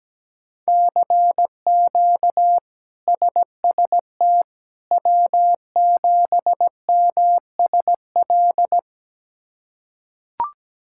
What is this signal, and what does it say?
Morse code, heard as a keyed tone, about 17 wpm, 700 Hz. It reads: CQ SST W7MSL
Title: K1USN SST QSO Practice - 17wpm